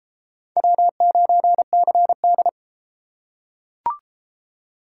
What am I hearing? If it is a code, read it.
Morse code, heard as a keyed tone, about 33 wpm, 700 Hz. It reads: W9CD